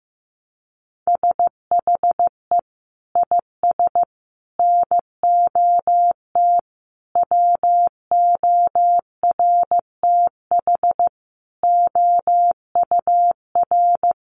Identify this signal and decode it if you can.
Morse code, heard as a keyed tone, about 15 wpm, 700 Hz. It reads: SHE IS NOT WORTH OUR
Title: Sentences from Top 1100 Words 15wpm